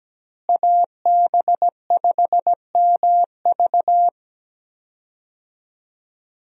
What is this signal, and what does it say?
Morse code, heard as a keyed tone, about 17 wpm, 700 Hz. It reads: AB5MV
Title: USA Call Signs - 5 Characters - Repeated 3x - 17wpm